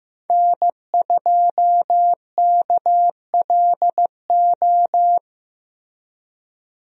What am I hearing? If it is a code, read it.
Morse code, heard as a keyed tone, about 15 wpm, 700 Hz. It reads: N2KLO
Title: USA Call Signs - 5 Characters - Repeated 3x - 15wpm